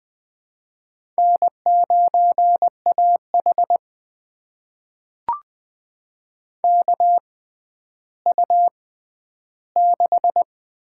Morse code, heard as a keyed tone, about 20 wpm, spaced as for 11 wpm, 700 Hz. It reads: N9AH KU6
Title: USA Call Signs - 4 Characters - 6x Character Spacing - 20wpm